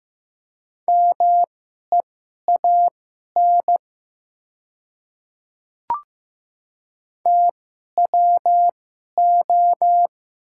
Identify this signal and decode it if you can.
Morse code, heard as a keyed tone, about 15 wpm, spaced as for 11 wpm, 700 Hz. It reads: MEAN TWO
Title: 2x Character Spacing - Top 200 Words - 15wpm